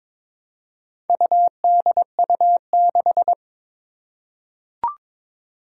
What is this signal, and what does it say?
Morse code, heard as a keyed tone, about 22 wpm, 700 Hz. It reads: UDU6
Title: Four Challenging Letter-Number Combinations 22wpm